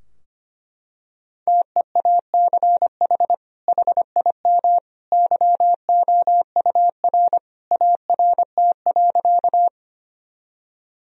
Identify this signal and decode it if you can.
Morse code, heard as a keyed tone, about 25 wpm, 700 Hz. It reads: TEACH HIM YOUR ART.
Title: Sentences from Top 1000 Words - Review Part 3 - 25wpm